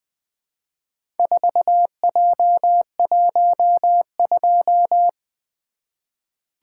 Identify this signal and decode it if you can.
Morse code, heard as a keyed tone, about 20 wpm, 700 Hz. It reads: 4J12
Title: Four Letter-Number Combinations - 20wpm